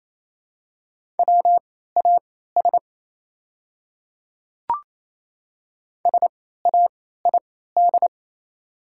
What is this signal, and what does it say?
Morse code, heard as a keyed tone, about 28 wpm, spaced as for 16 wpm, 700 Hz. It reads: WAS SAID
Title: 3x Character Spacing - Top 500 Words - 28wpm